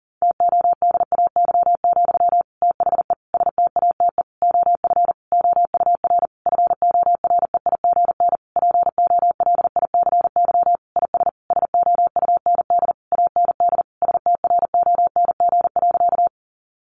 Morse code, heard as a keyed tone, about 37 wpm, 700 Hz. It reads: TODAY, THE STATE OF OUR FOREIGN POLICY IS SOUND AND STRONG.
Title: Sentences from Top 1400 Words 40wpm